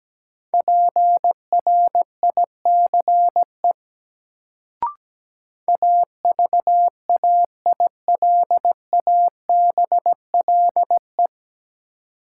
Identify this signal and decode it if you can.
Morse code, heard as a keyed tone, about 17 wpm, 700 Hz. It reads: PRICE AVAILABLE
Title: Sentences from Top 600 Words - New Word Drill - 17wpm